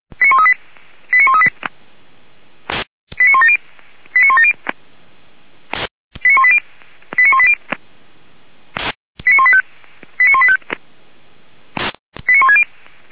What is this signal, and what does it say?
Five-tone selective call (ZVEI), heard as quick runs of five tones, tones of about 80 ms. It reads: 89138 89170 89170 89180 89180 89185 89185 89160